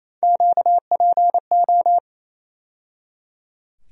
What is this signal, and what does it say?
Morse code, heard as a keyed tone, about 28 wpm, 700 Hz. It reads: QPO
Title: USA Call Sign Postfixes - 3 Letters 28wpm